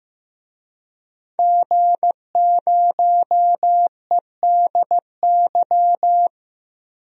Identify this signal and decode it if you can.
Morse code, heard as a keyed tone, about 15 wpm, 700 Hz. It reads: G0EDY
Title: DXCC-100 Call Signs - 5 Characters - 15wpm